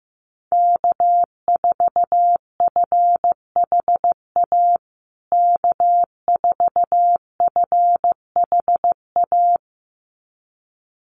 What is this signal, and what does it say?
Morse code, heard as a keyed tone, about 15 wpm, 700 Hz. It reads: K4FHA K4FHA
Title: USA Call Signs Repeated 2x 15wpm